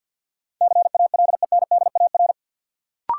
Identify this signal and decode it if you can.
Morse code, heard as a keyed tone, about 52 wpm, 700 Hz. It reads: CALENDAR
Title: Sentences from Top 1600 Words - New Word Drill - 50wpm